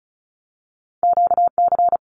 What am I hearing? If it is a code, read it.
Morse code, heard as a keyed tone, about 36 wpm, 700 Hz. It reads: QC
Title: Canadian Provinces and Territories 35wpm